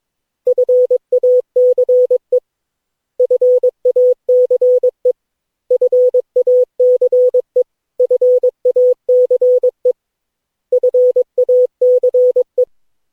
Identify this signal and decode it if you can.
Morse code, heard as a keyed tone, about 22 wpm, 500 Hz. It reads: FACE FACE FACE FACE FACE
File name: cw / Session9 cw training